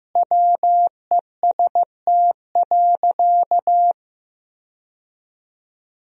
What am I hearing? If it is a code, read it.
Morse code, heard as a keyed tone, about 16 wpm, 700 Hz. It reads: WEST.
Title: 4x Word-Spacing - Sentences from Top 400 Words - 15wpm